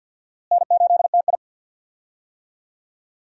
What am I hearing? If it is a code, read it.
Morse code, heard as a keyed tone, about 47 wpm, 700 Hz. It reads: N8TI
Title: USA Call Signs - 4 Characters - Repeated 2x - 50wpm